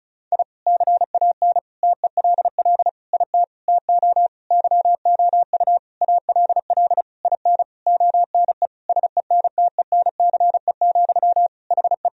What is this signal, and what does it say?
Morse code, heard as a keyed tone, about 36 wpm, 700 Hz. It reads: I CAN TELL IT TO YOU ALL IN ONE SENTENCE, HE
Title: Sentences from Top 1000 Words - Review Part 1 - 35wpm